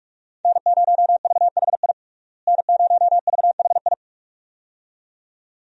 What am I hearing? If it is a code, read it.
Morse code, heard as a keyed tone, about 46 wpm, 700 Hz. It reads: N0VHI N0VHI
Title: USA Call Signs - 5 Characters - Repeated 2x - 45wpm